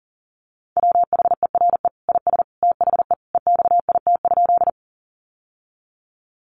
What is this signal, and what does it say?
Morse code, heard as a keyed tone, about 40 wpm, 700 Hz. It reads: WHERE IS THE EXIT?
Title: Sentences from Top 1100 Words 40wpm